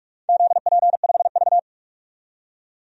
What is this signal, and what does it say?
Morse code, heard as a keyed tone, about 46 wpm, 700 Hz. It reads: ZP5V
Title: DXCC-100 Call Signs - 4 Characters - 45wpm